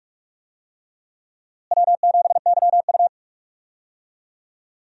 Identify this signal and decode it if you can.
Morse code, heard as a keyed tone, about 46 wpm, 700 Hz. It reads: W7YU
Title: USA Call Signs - 4 Characters - Repeated 3x - 45wpm